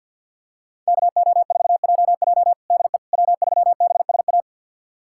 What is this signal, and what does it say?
Morse code, heard as a keyed tone, about 53 wpm, 700 Hz. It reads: KO4JJ DE W3BSA
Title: USA Callsign DE Callsign - 5 Characters - 50wpm